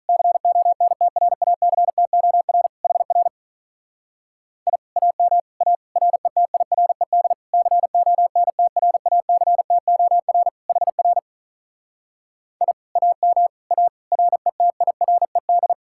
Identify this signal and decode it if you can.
Morse code, heard as a keyed tone, about 46 wpm, 700 Hz. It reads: CONTRACTOR HR I AM A RETIRED CONTRACTOR HR I AM A RETIRED